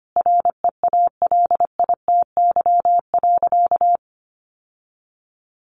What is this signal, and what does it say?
Morse code, heard as a keyed tone, about 25 wpm, 700 Hz. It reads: REALITY.
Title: Sentences from Top 1200 Words 25wpm